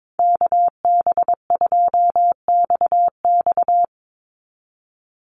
Morse code, heard as a keyed tone, about 22 wpm, 700 Hz. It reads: KB2XX
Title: NA SOTA Chaser Callsigns 22wpm